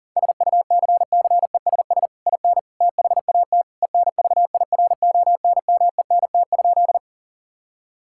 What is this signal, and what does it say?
Morse code, heard as a keyed tone, about 37 wpm, 700 Hz. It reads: SUCCESS IN THAT ENVIRONMENT?
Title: Sentences from Top 1000 Words - Review Part 5 - 40wpm